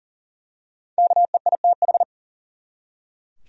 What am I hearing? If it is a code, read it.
Morse code, heard as a keyed tone, about 40 wpm, 700 Hz. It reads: KEITH
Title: USA Common Names - 40wpm